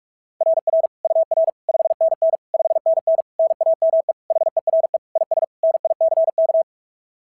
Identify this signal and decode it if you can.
Morse code, heard as a keyed tone, about 41 wpm, 690 Hz. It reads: RR UR 5NN 5NN NAME HERE IS NICK
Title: SOTA QSO Practice 45wpm